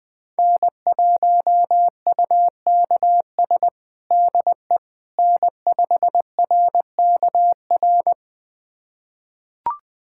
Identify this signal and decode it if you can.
Morse code, heard as a keyed tone, about 20 wpm, 700 Hz. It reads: N1UKS DE N5RKR
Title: USA Callsign DE Callsign - 5 Characters - 20wpm